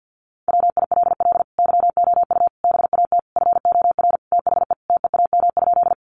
Morse code, heard as a keyed tone, about 50 wpm, 700 Hz. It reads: WILL YOU BAT FOR THE TEAM?
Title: Sentences from Top 1500 Words 50wpm